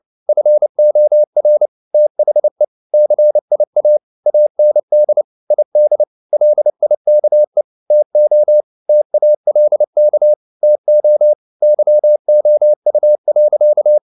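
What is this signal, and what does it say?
Morse code, heard as a keyed tone, about 29 wpm, 600 Hz. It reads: FOR THE CIA AND ID LIKE TO TALK TO YOU.
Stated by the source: Morse code at 29 wpm, sending Ken Follett's novel Never in English, from CW med Gnister.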